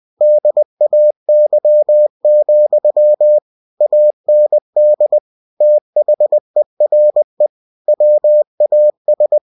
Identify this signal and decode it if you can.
Morse code, heard as a keyed tone, about 20 wpm, 600 Hz. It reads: DAY, AND THERE WAS
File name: cw-books / winnie-the-pooh / winnie-20wpm-page-0041